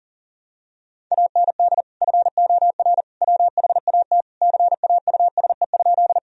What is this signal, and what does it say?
Morse code, heard as a keyed tone, about 40 wpm, 700 Hz. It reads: AND FOR WHAT CAUSE?
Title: Sentences from Top 1000 Words - Review Part 1 - 40wpm